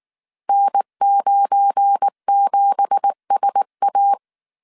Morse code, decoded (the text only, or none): N97SA